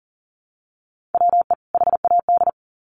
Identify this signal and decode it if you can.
Morse code, heard as a keyed tone, about 40 wpm, 700 Hz. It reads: WE HAD
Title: Sets of 2 Words 40wpm